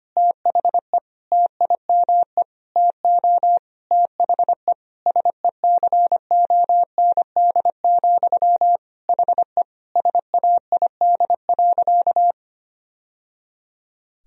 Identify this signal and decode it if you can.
Morse code, heard as a keyed tone, about 25 wpm, 700 Hz. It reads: THE TIME TO THE SECOND, HE SAID.
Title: Sentences from Top 1000 Words - Review Part 1 - 25wpm